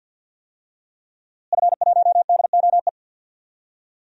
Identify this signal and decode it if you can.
Morse code, heard as a keyed tone, about 48 wpm, 700 Hz. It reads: F1DOE
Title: DXCC-100 Call Signs - 5 Characters - Repeated 2x - 50wpm